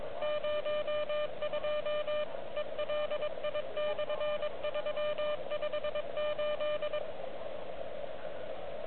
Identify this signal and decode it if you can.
Morse code, heard as a keyed tone, about 23 wpm, 2400 Hz, with radio noise. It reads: 02 ELI/358
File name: OK QRP 2022 / Zvuk / REC30